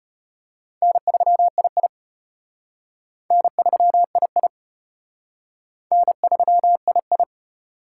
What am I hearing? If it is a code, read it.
Morse code, heard as a keyed tone, about 33 wpm, 700 Hz. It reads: N3II N3II N3II